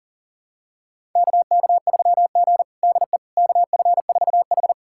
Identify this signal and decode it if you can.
Morse code, heard as a keyed tone, about 37 wpm, 700 Hz. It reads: KK3G DE KF4H
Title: USA Callsign DE Callsign - 4 Characters - 40wpm